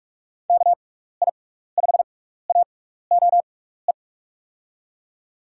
Morse code, heard as a keyed tone, about 42 wpm, spaced as for 15 wpm, 700 Hz. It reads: KI5AOE
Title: USA Call Signs - 6 Characters - 6x Character Spacing - 45wpm